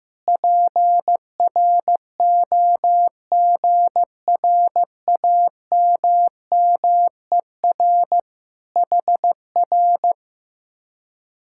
Morse code, heard as a keyed tone, about 15 wpm, 700 Hz. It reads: PROGRAMMER HR
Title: Speed-Racing Common QSO Phrases Encore - 15wpm